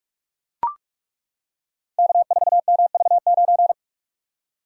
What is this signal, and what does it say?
Morse code, heard as a keyed tone, about 42 wpm, 700 Hz. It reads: K4MV9
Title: Code Groups - 5 Letter:Numbers - 45wpm